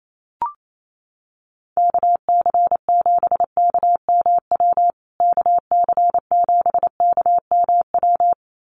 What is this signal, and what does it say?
Morse code, heard as a keyed tone, about 28 wpm, 700 Hz. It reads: KC7KMW KC7KMW
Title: USA Call Signs Repeated 3x 28wpm